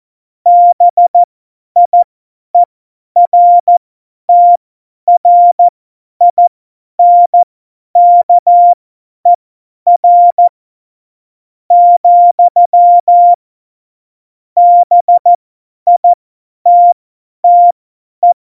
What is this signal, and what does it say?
Morse code, heard as a keyed tone, about 14 wpm, spaced as for 10 wpm, 700 Hz. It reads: BIERTRINKER, BITTE
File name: Morsekurs / Witze / Witz 18. Februar 2020, 50 F70